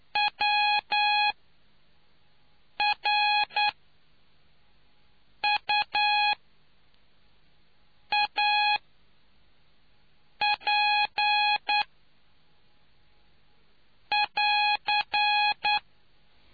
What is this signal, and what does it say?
Morse code, heard as a keyed tone, about 9 wpm, spaced as for 4 wpm, 830 Hz. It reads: WRUAP<AR>